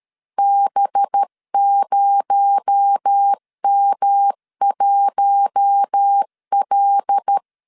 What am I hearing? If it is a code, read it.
Morse code, heard as a keyed tone, about 13 wpm, 800 Hz. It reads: B0M1L
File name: LEZIONI / 7 GF9Z / 7 G 12wpm 2055